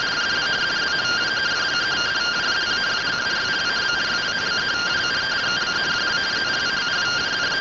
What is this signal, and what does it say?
Two-tone teleprinter signal, about 170 Hz shift.
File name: findings / AXIAL fr1 3mm.gen-09 rtty rx